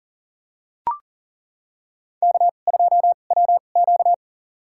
Morse code, heard as a keyed tone, about 42 wpm, spaced as for 29 wpm, 690 Hz. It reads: K2WQ
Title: USA Call Signs - 2x Character Spacing - 40wpm